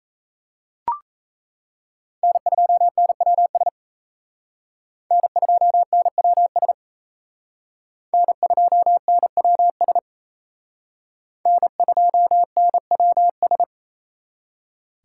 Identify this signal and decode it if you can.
Morse code, heard as a keyed tone, about 35 wpm, 700 Hz. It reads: N2NWS N2NWS N2NWS N2NWS
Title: Speed Racing - USA 1x3 Call Signs 28wpm